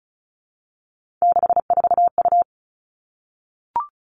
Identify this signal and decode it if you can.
Morse code, heard as a keyed tone, about 36 wpm, 700 Hz. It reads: 64U
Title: Three Challenging Letter-Number Combinations 35wpm